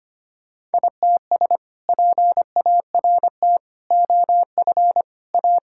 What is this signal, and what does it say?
Morse code, heard as a keyed tone, about 25 wpm, 700 Hz. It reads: ITS PART OF A
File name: practice / 25wpm / Sentences with 2-Word Phrases - 25wpm